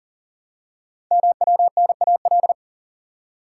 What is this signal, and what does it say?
Morse code, heard as a keyed tone, about 39 wpm, 690 Hz. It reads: MWNAL
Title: Code Groups - 5 Letters - 40wpm